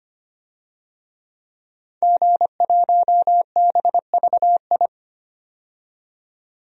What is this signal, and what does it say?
Morse code, heard as a keyed tone, about 25 wpm, 700 Hz. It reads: G1BVI
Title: DXCC-100 Call Signs - 5 Characters - 25wpm